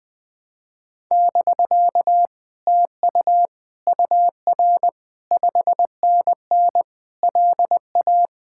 Morse code, heard as a keyed tone, about 20 wpm, 700 Hz. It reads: <BK> TU UR 5NN LA
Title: Speed-Racing Common QSO Phrases - 20wpm